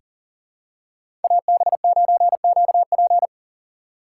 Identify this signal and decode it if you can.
Morse code, heard as a keyed tone, about 40 wpm, 700 Hz. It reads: AB9QP
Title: USA Call Signs - 5 Characters - 3x Character Spacing - 40wpm